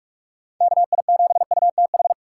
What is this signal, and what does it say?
Morse code, heard as a keyed tone, about 42 wpm, 700 Hz. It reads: KI7UTH
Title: USA Call Signs - 6 Characters - 3x Character Spacing - 45wpm